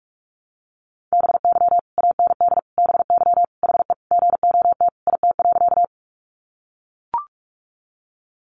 Morse code, heard as a keyed tone, about 44 wpm, 700 Hz. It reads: BY AND BY HE GOT IT.
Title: Sentences from Top 300 Words 45wpm